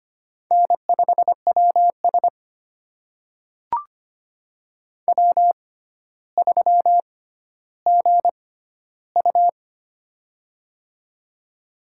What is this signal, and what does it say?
Morse code, heard as a keyed tone, about 25 wpm, spaced as for 13 wpm, 700 Hz. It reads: N5WS W3GU
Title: USA Call Signs - 4 Characters - 6x Character Spacing - 25wpm